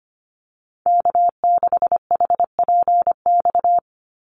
Morse code, heard as a keyed tone, about 25 wpm, 700 Hz. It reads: K6HPX
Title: NA SOTA Chaser Callsigns 25wpm